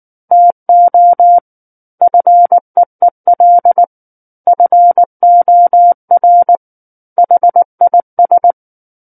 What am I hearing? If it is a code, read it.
Morse code, heard as a keyed tone, about 19 wpm, 700 Hz. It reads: TO FEEL FOR HIS
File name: Book One / 10-Chapter 10 - 19 WPM